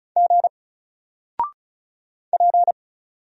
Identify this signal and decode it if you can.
Morse code, heard as a keyed tone, about 35 wpm, 700 Hz. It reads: G P
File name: Single-Letter-Number-35wpm